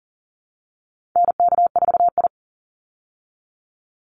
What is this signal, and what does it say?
Morse code, heard as a keyed tone, about 40 wpm, 700 Hz. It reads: NK4I
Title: Call Signs Repeated 2x - 40wpm